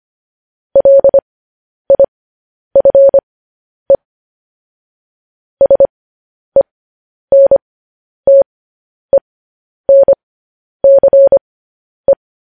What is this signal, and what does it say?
Morse code, heard as a keyed tone, about 25 wpm, spaced as for 10 wpm, 550 Hz. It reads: LIFE SENTENCE
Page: news headlines in morse code at 10 wpm